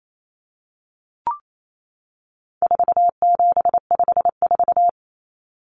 Morse code, heard as a keyed tone, about 28 wpm, 700 Hz. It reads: 4754
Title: Common Cut Numbers - 4-Digit 28wpm